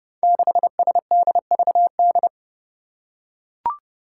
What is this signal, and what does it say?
Morse code, heard as a keyed tone, about 32 wpm, 700 Hz. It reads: 6SDVD
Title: Five Challenging Letter-Number Combinations 30wpm